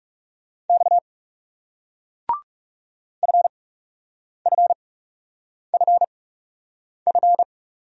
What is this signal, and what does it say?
Morse code, heard as a keyed tone, about 37 wpm, spaced as for 8 wpm, 700 Hz. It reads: X FFFF
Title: Speed-Racing Single Letters 30wpm